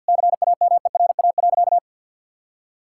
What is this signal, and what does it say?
Morse code, heard as a keyed tone, about 52 wpm, 700 Hz. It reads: CAMERA.
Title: Sentences from Top 1000 Words - Review Part 5 - 50wpm